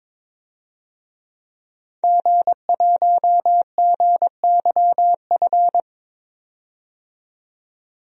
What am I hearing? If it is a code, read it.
Morse code, heard as a keyed tone, about 21 wpm, 700 Hz. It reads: G1GYF
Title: DXCC-100 Call Signs - 5 Characters - 22wpm